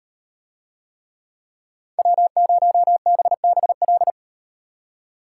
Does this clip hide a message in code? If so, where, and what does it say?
Yes: Morse code from 2.0 to 4.1 s, heard as a keyed tone, about 38 wpm, 700 Hz. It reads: W0BBL